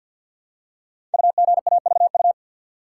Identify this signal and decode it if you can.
Morse code, heard as a keyed tone, about 50 wpm, 700 Hz. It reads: UGAVU
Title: Code Groups - 5 Letter:Numbers - 50wpm